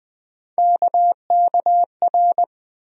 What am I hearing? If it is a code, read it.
Morse code, heard as a keyed tone, about 20 wpm, 700 Hz. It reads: KKR
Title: USA Call Sign Postfixes - 3 Letters 20wpm